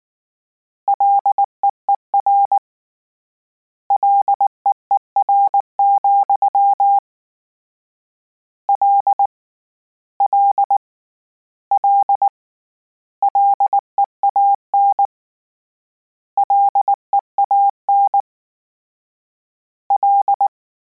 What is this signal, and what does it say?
Morse code, heard as a keyed tone, about 19 wpm, 800 Hz. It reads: LEER LEER, L L L LEAN LEAN L